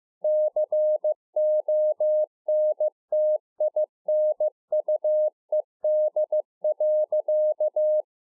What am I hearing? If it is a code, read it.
Morse code, heard as a keyed tone, about 15 wpm, 610 Hz. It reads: CONTINUED.